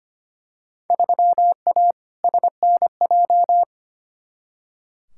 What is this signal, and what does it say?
Morse code, heard as a keyed tone, about 25 wpm, 700 Hz. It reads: 3A SNJ
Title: Field Day Exchanges - 25wpm